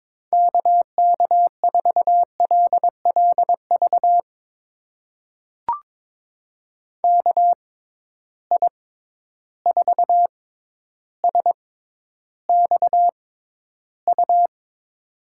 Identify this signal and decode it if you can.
Morse code, heard as a keyed tone, about 22 wpm, spaced as for 11 wpm, 700 Hz. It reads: KK4LLV KI4SXU
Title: USA Call Signs - 6 Characters - 6x Character Spacing - 22wpm